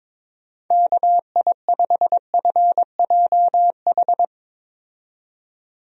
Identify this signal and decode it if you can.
Morse code, heard as a keyed tone, about 21 wpm, 700 Hz. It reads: KI5FJH
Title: USA Call Signs - 6 Characters - 22wpm